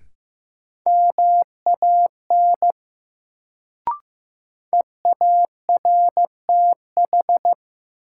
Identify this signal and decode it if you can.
Morse code, heard as a keyed tone, about 15 wpm, 700 Hz. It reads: MAN EARTH
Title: Sentences from Top 200 Words - New Word Drill - 15wpm